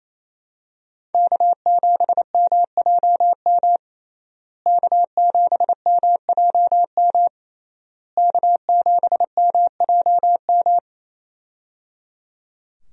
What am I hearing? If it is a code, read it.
Morse code, heard as a keyed tone, about 28 wpm, 700 Hz. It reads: K7MJM K7MJM K7MJM
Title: USA Call Signs - 5 Characters - Repeated 3x - 28wpm